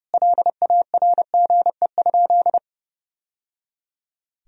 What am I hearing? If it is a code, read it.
Morse code, heard as a keyed tone, about 30 wpm, 700 Hz. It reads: LARGE?
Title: Sentences from Top 1000 Words - Review Part 1 - 30wpm